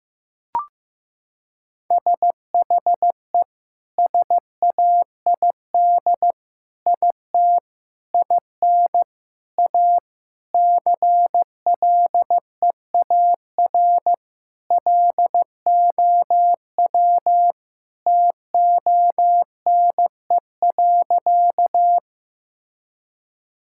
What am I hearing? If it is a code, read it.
Morse code, heard as a keyed tone, about 16 wpm, 700 Hz. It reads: SHE SAID IT IN A CLEAR LOW TONE.
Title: Sentences from Top 1600 Words 15wpm